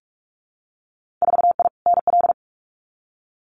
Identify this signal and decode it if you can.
Morse code, heard as a keyed tone, about 46 wpm, 700 Hz. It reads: 4I NL
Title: Winter Field Day Exchanges Encore 45wpm